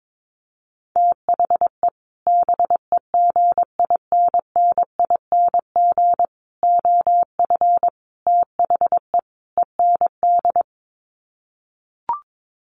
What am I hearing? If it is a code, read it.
Morse code, heard as a keyed tone, about 22 wpm, 700 Hz. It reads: THE BEGINNING OF THE END
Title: Sets of 5 Words Encore2 22wpm